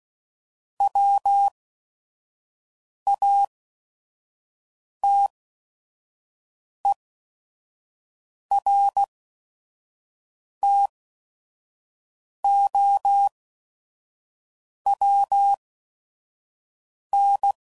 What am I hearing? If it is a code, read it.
Morse code, heard as a keyed tone, about 16 wpm, spaced as for 5 wpm, 800 Hz. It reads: WATERTOWN
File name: CW Code Groups @5WPM / fromSM's / Cities